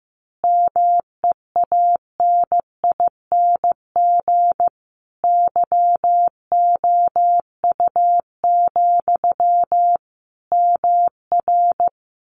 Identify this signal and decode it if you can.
Morse code, heard as a keyed tone, about 15 wpm, 700 Hz. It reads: MEANING YOU, MR
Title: A Sound Of Thunder 15wpm